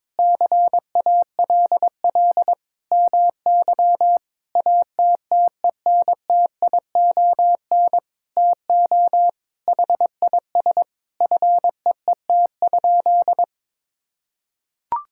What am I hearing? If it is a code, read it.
Morse code, heard as a keyed tone, about 22 wpm, 700 Hz. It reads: CALL MY ATTENTION TO HIS FEET?
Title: Sentences from Top 1000 Words - Review Part 6 - 22wpm